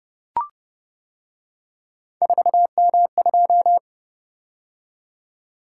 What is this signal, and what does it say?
Morse code, heard as a keyed tone, about 29 wpm, 700 Hz. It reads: 4M2